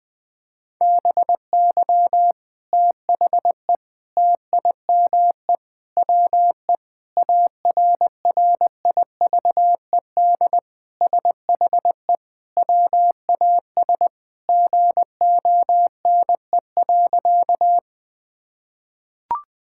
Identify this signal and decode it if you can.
Morse code, heard as a keyed tone, about 20 wpm, 700 Hz. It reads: BY THE TIME WE ARRIVED SHE WAS GONE.
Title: Sentences with 3-Word Phrases - 20wpm